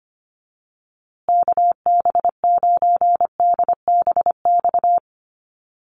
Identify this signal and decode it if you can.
Morse code, heard as a keyed tone, about 25 wpm, 700 Hz. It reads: KB9DBX
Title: USA Call Signs Repeated 3x 25wpm